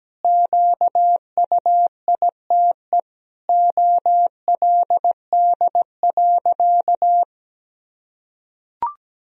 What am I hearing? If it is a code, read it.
Morse code, heard as a keyed tone, about 17 wpm, 700 Hz. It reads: QUITE OLD.
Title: Sentences from Top 1000 Words - Review Part 3 - 17wpm